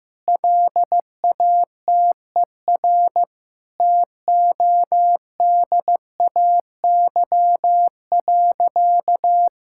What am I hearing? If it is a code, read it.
Morse code, heard as a keyed tone, about 15 wpm, 700 Hz. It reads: LATER TODAY.
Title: Sentences from Top 1000 Words - Review Part 5 - 15wpm